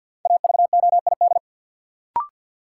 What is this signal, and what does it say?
Morse code, heard as a keyed tone, about 50 wpm, 700 Hz. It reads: AVOID
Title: Sentences from Top 1200 Words - New Word Drill Alt - 50wpm